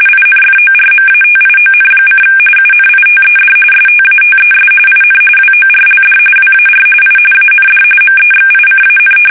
RTTY, about 45.45 baud. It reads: INFORMATION: UNITED STATES IN WORLD WAR I AND MEXICAN
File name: fsk850